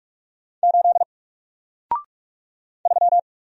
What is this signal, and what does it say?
Morse code, heard as a keyed tone, about 44 wpm, 700 Hz. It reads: 8 3
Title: Single Character (Letter:Number:Punctuation:Prosign) 45wpm